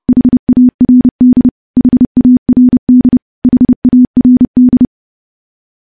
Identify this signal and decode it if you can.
Morse code, heard as a keyed tone, about 30 wpm, 250 Hz. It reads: HARD HARD HARD
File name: morse / morse-training-30WPM